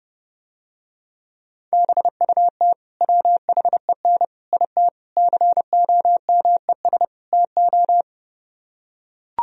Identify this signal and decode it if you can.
Morse code, heard as a keyed tone, about 31 wpm, 700 Hz. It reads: BUT WHEN IT COMES TO